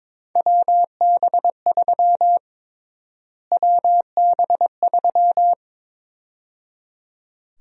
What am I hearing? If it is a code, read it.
Morse code, heard as a keyed tone, about 22 wpm, 700 Hz. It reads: WB3 WB3
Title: USA Call Sign Prefixes - Repeated 2x - 22wpm